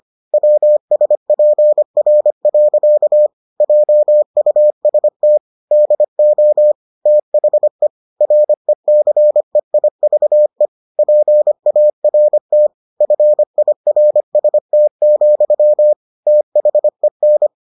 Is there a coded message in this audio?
Yes: Morse code from 0.3 to 17.5 s, heard as a keyed tone, about 25 wpm, 600 Hz. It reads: WSPR. JUST DO THE RECEIVE PART FIRST, THEN